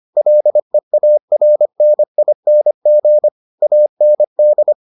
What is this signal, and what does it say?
Morse code, heard as a keyed tone, about 25 wpm, 600 Hz. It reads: LEARNING AND